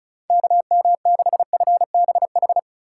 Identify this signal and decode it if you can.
Morse code, heard as a keyed tone, about 34 wpm, 690 Hz. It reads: KM6FBH
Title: USA Call Signs - 6 Characters - 3x Character Spacing - 35wpm